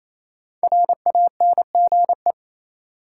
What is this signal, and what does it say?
Morse code, heard as a keyed tone, about 28 wpm, 700 Hz. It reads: RANGE
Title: Sets of 4 Words - New Word Drill - 28wpm